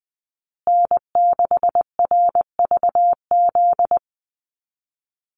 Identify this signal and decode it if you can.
Morse code, heard as a keyed tone, about 20 wpm, 700 Hz. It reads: N6RVZ
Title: USA Call Signs Repeated 3x 20wpm